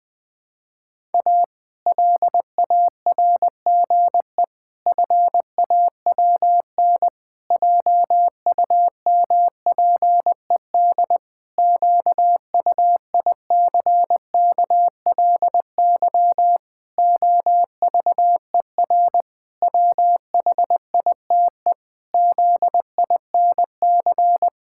Morse code, heard as a keyed tone, about 20 wpm, 700 Hz. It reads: A LARGE FAWN JUMPED QUICKLY OVER WHITE ZINC
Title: Pangram Challenge v2 - 20wpm